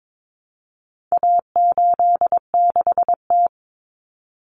Morse code, heard as a keyed tone, about 22 wpm, 700 Hz. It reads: A86T
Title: Common Cut Numbers - 4-Digit 22wpm